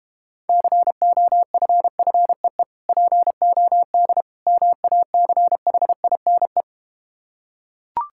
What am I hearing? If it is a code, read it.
Morse code, heard as a keyed tone, about 32 wpm, 700 Hz. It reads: COFFEE POD MACHINE